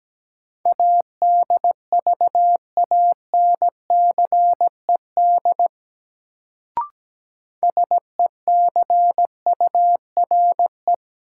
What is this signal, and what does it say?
Morse code, heard as a keyed tone, about 17 wpm, 700 Hz. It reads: ADVANCED SECURE
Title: Sentences from Top 1200 Words - New Word Drill - 17wpm